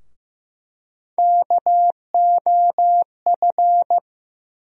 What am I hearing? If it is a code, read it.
Morse code, heard as a keyed tone, about 15 wpm, 700 Hz. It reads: KOF
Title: USA Call Sign Postfixes - 3 Letters 15wpm